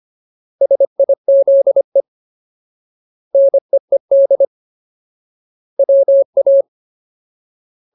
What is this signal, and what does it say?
Morse code, heard as a keyed tone, about 25 wpm, 550 Hz. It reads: SIZE NEED WA
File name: Ultimate Word List / 25WPM Ultimate